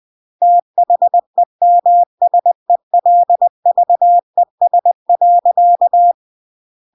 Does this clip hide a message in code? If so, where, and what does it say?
Yes: Morse code from 0.4 to 6.1 s, heard as a keyed tone, about 20 wpm, 700 Hz. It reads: THEMSELVES.